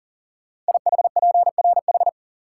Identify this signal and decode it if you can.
Morse code, heard as a keyed tone, about 42 wpm, 690 Hz. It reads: IHPRH
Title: Code Groups - 5 Letters - 40wpm